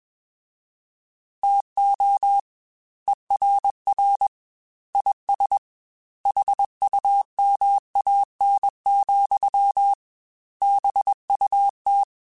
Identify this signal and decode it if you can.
Morse code, heard as a keyed tone, about 21 wpm, 800 Hz. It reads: TO ERR IS HUMAN, BUT
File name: Chuckles / Chuckles-mp3s / HumanErr @20